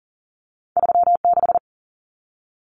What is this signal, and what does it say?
Morse code, heard as a keyed tone, about 40 wpm, 700 Hz. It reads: 36
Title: Common Cut Numbers - 2-Digit 40wpm